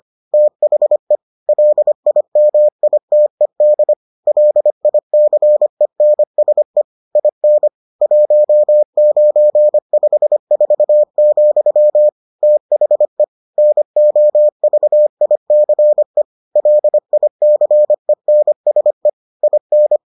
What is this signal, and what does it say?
Morse code, heard as a keyed tone, about 25 wpm, 600 Hz. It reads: THE LIMITED LICENSE IN 1954, THE NOVICE LICENSE IN